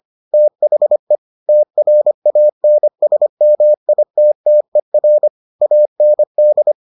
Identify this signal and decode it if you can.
Morse code, heard as a keyed tone, about 25 wpm, 600 Hz. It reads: THE TRANSMITTER AND